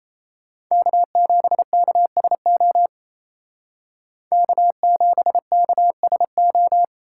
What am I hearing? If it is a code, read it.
Morse code, heard as a keyed tone, about 34 wpm, 700 Hz. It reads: K7KSO K7KSO
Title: Speed Racing - USA 1x3 Call Signs 28wpm